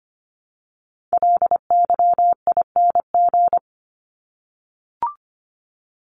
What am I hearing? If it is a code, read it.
Morse code, heard as a keyed tone, about 25 wpm, 700 Hz. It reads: LYING
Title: ING Suffix - Top 100 Words with Base Words 25wpm